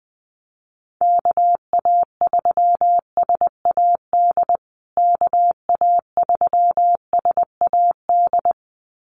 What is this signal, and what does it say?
Morse code, heard as a keyed tone, about 20 wpm, 700 Hz. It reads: KA3SAD KA3SAD
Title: USA Call Signs Repeated 2x 20wpm